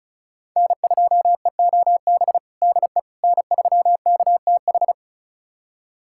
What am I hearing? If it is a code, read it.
Morse code, heard as a keyed tone, about 34 wpm, 700 Hz. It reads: N2EOB DE N3KTH
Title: USA Callsign DE Callsign - 5 Characters - 35wpm